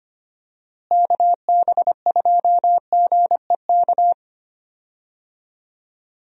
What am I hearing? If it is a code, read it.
Morse code, heard as a keyed tone, about 25 wpm, 700 Hz. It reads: KB2GEK
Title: USA Call Signs - 6 Characters - Repeated 2x - 25wpm